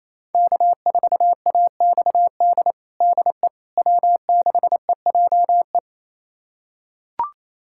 Morse code, heard as a keyed tone, about 28 wpm, 700 Hz. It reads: K4AXD DE W6EJE
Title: USA Callsign DE Callsign 28wpm